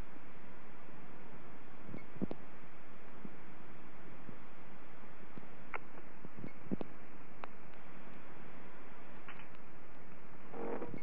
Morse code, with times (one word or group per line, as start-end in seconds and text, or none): none